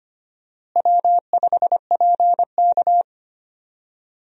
Morse code, heard as a keyed tone, about 25 wpm, 700 Hz. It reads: W5PK
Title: USA Call Signs - 4 Characters - Repeated 3x - 25wpm